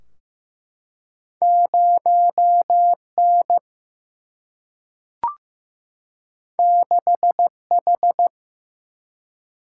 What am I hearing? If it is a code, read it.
Morse code, heard as a keyed tone, about 15 wpm, 700 Hz. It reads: 0N 6H
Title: Single-Letter Single-Number 15wpm